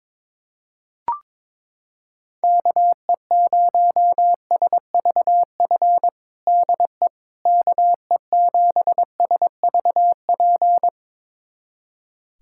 Morse code, heard as a keyed tone, about 22 wpm, 700 Hz. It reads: KE0SVF DE KE7SVP
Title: USA Callsign DE Callsign - 6 Characters - 22wpm